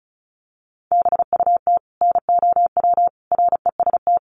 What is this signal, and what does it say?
Morse code, heard as a keyed tone, about 34 wpm, 700 Hz. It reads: BUT NOW REST
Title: Sentences from Top 1100 Words 35wpm